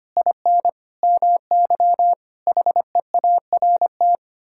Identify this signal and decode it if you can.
Morse code, heard as a keyed tone, about 25 wpm, 700 Hz. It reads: IN MY HEART
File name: Sets of 9 Words - 25wpm